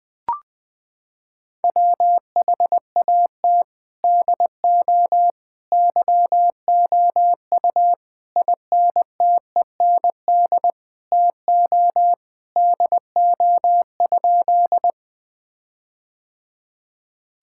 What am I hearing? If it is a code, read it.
Morse code, heard as a keyed tone, about 21 wpm, 700 Hz. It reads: WHAT DO YOU INTEND TO DO?
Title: Sentences from Top 1600 Words 20wpm